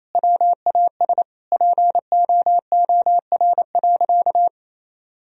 Morse code, heard as a keyed tone, about 28 wpm, 700 Hz. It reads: WAS POOR.
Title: Sentences from Top 1000 Words - Review Part 3 - 28wpm